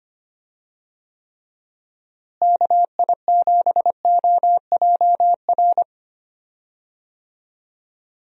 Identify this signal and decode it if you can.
Morse code, heard as a keyed tone, about 25 wpm, 700 Hz. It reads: KI7OJR
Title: USA Call Signs - 6 Characters - Repeated 2x - 25wpm